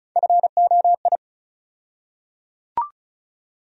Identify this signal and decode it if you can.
Morse code, heard as a keyed tone, about 34 wpm, 700 Hz. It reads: FOI
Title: USA Call Sign Postfixes - 3 Letters 35wpm